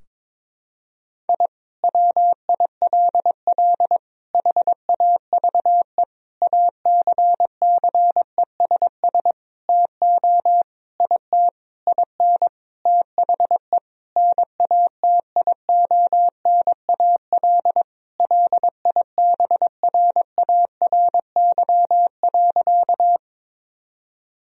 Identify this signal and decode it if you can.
Morse code, heard as a keyed tone, about 22 wpm, 700 Hz. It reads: I WILL HAVE ACCESS TO IT IN THE NATIONAL LIBRARY.
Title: Sentences from Top 1000 Words - Review Part 4 - 22wpm